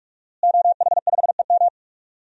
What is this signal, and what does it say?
Morse code, heard as a keyed tone, about 45 wpm, 700 Hz. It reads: OH5EM
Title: DXCC-100 Call Signs - 5 Characters - Repeated 3x - 45wpm